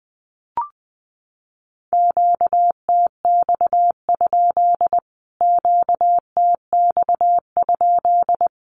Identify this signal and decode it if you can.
Morse code, heard as a keyed tone, about 20 wpm, 700 Hz. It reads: QTX? QTX?
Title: Q-Codes Repeated 3x 20wpm